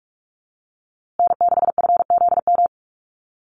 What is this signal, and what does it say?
Morse code, heard as a keyed tone, about 44 wpm, 700 Hz. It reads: N6FZM
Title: USA Call Signs Repeated 2x 45wpm